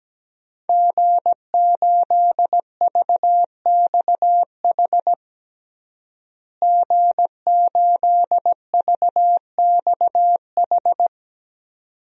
Morse code, heard as a keyed tone, about 17 wpm, 700 Hz. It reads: G8VXH G8VXH
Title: DXCC-100 Call Signs - 5 Characters - Repeated 2x - 17wpm